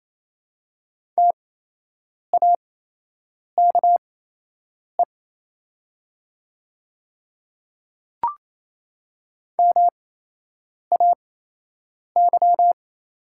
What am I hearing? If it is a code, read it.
Morse code, heard as a keyed tone, about 28 wpm, spaced as for 8 wpm, 700 Hz. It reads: TAKE MAY
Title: 8x Character Spacing - Top 200 Words - 28wpm